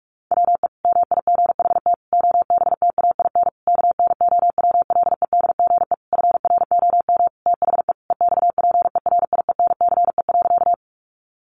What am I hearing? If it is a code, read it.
Morse code, heard as a keyed tone, about 44 wpm, 700 Hz. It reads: WE MIGHT OBTAIN KNOWLEDGE FROM THE EXPERIENCE.
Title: Sentences from Top 1300 Words 45wpm